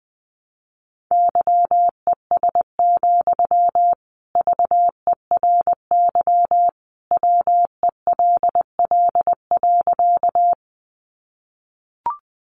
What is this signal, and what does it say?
Morse code, heard as a keyed tone, about 20 wpm, 700 Hz. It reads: YES, VERY WELL.
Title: Sentences from Top 500 Words 20wpm